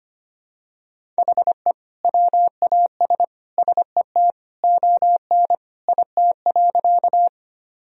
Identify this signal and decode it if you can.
Morse code, heard as a keyed tone, about 25 wpm, 700 Hz. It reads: HE WAS SET ON IT.
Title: Sentences from Top 1000 Words - Review Part 1 - 25wpm